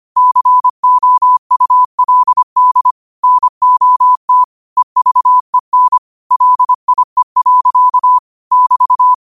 Morse code, heard as a keyed tone, about 25 wpm, 1000 Hz. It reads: COULD NOT EVEN LIE. <BT>
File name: qotdfiles / 25wpm